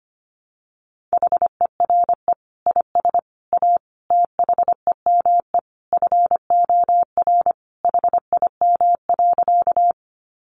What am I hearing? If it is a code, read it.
Morse code, heard as a keyed tone, about 25 wpm, 700 Hz. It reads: HERE IS A THEME FOR HIM.
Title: Sentences from Top 1100 Words 25wpm